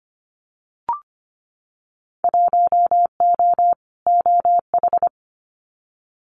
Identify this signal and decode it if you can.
Morse code, heard as a keyed tone, about 25 wpm, 700 Hz. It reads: 1O OH
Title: Winter Field Day Exchanges 25wpm